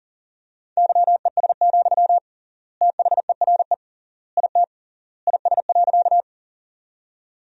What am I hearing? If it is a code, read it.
Morse code, heard as a keyed tone, about 40 wpm, 700 Hz. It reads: YES, THERE IT IS.
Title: 3x Word-Spacing - Sentences from Top 500 Words - 40wpm